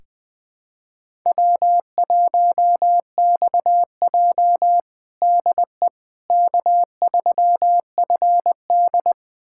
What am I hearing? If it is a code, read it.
Morse code, heard as a keyed tone, about 20 wpm, 700 Hz. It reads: W1XJ DE K3FD
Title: USA Callsign DE Callsign - 4 Characters - 20wpm